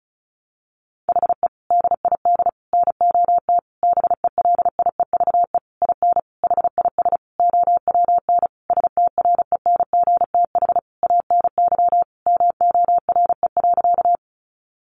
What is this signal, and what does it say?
Morse code, heard as a keyed tone, about 34 wpm, 700 Hz. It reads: HE DID NOT BELIEVE IN HIS OWN STRENGTH ANY MORE.
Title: Sentences from Top 1300 Words 35wpm